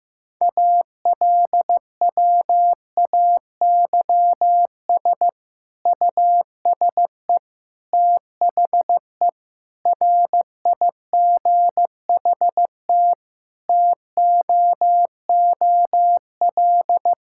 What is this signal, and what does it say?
Morse code, heard as a keyed tone, about 15 wpm, 700 Hz. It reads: ALWAYS USE THE RIGHT TOOL
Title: Sentences from Top 1000 Words - Review Part 6 - 15wpm